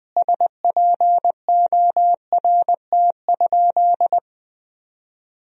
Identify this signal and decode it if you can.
Morse code, heard as a keyed tone, about 20 wpm, 700 Hz. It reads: SPORT?
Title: Sentences from Top 1000 Words - Review Part 4 - 20wpm